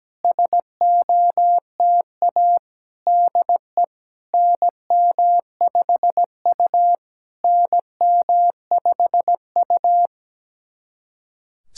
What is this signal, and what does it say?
Morse code, heard as a keyed tone, about 17 wpm, 700 Hz. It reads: SOTA DE NM5U NM5U
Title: SOTA QSO Practice 17wpm